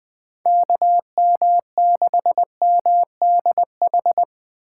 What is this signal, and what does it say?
Morse code, heard as a keyed tone, about 20 wpm, 700 Hz. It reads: KM6MDH
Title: USA Call Signs - 6 Characters - 4x Character Spacing - 20wpm